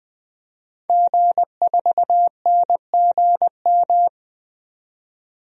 Morse code, heard as a keyed tone, about 21 wpm, 700 Hz. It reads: G4NGM
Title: DXCC-100 Call Signs - 5 Characters - 20wpm